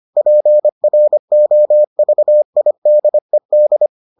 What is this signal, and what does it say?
Morse code, heard as a keyed tone, about 25 wpm, 600 Hz. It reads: PROVIDED